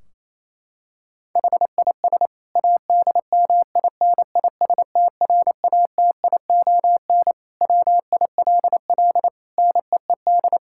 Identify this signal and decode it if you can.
Morse code, heard as a keyed tone, about 28 wpm, 700 Hz. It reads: HIS ADMINISTRATION WILL NEED
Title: Sentences from Top 1600 Words 28wpm